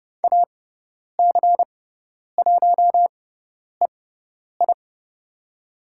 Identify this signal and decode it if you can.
Morse code, heard as a keyed tone, about 30 wpm, spaced as for 10 wpm, 700 Hz. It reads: AC1EI
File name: Lesson 18g - Letter m - Call Signs 30wpm